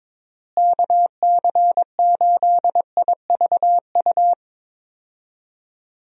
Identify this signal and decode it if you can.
Morse code, heard as a keyed tone, about 22 wpm, 700 Hz. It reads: KC8IVU
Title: USA Call Signs - 6 Characters - Repeated 3x - 22wpm